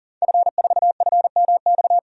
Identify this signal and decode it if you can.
Morse code, heard as a keyed tone, about 40 wpm, 700 Hz. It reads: F4FMX
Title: DXCC-100 Call Signs - 5 Characters - 40wpm